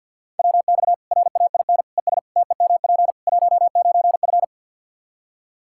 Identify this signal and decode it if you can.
Morse code, heard as a keyed tone, about 47 wpm, 700 Hz. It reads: WX RAIN ES TEMP 19F
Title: Practice QSO - Exchange 2 - Info - 50wpm